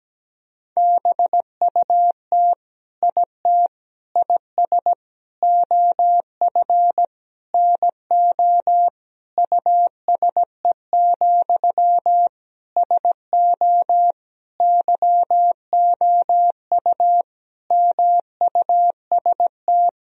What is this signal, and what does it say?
Morse code, heard as a keyed tone, about 17 wpm, 700 Hz. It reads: BUT IT IS OF NO USE, SO YOU MUST
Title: Sentences from Top 1000 Words - Review Part 2 - 17wpm